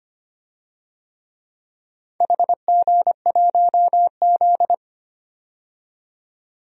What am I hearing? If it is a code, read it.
Morse code, heard as a keyed tone, about 25 wpm, 700 Hz. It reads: HG1Z
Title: DXCC-100 Call Signs - 4 Characters - 25wpm